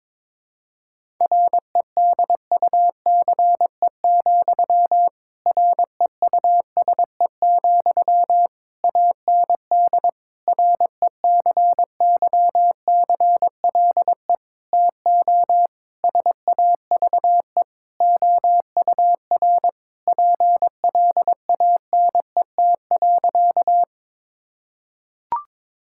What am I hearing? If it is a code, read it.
Morse code, heard as a keyed tone, about 22 wpm, 700 Hz. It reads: REDUCE, REUSE, AND RECYCLE TO SAVE OUR PLANET.
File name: Difficult Sentences with Hints - 22wpm